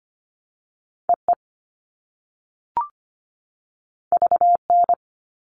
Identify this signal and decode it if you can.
Morse code, heard as a keyed tone, about 25 wpm, 700 Hz. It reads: EE VN
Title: All Cut Numbers - 2-Digit 25wpm